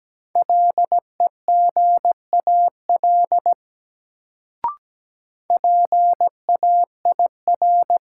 Morse code, heard as a keyed tone, about 17 wpm, 700 Hz. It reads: LEGAL PAIR
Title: Sentences from Top 1200 Words - New Word Drill Alt - 17wpm